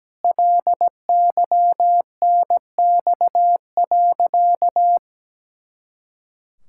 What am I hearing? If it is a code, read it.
Morse code, heard as a keyed tone, about 17 wpm, 700 Hz. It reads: LYNX.
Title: Pangram Challenge v2 - 17wpm